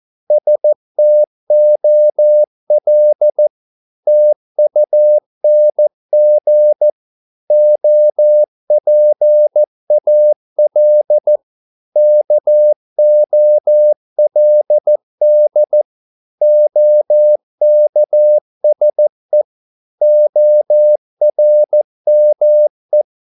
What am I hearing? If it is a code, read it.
Morse code, heard as a keyed tone, about 14 wpm, 600 Hz. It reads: STOL TUNG OPAL KOLD OKSE ORME
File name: Korte ord DK 14wpm